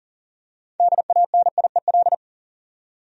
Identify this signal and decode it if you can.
Morse code, heard as a keyed tone, about 40 wpm, 700 Hz. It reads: DANIEL
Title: USA Common Names - 40wpm